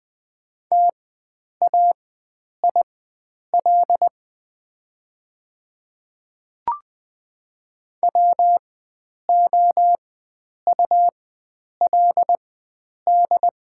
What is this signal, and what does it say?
Morse code, heard as a keyed tone, about 20 wpm, spaced as for 9 wpm, 700 Hz. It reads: TAIL WOULD
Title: 4x Character Spacing - Top 500 Words - 20wpm